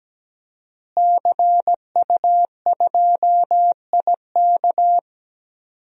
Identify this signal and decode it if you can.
Morse code, heard as a keyed tone, about 17 wpm, 700 Hz. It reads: CU2IK
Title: DXCC-100 Call Signs - 5 Characters - Repeated 2x - 17wpm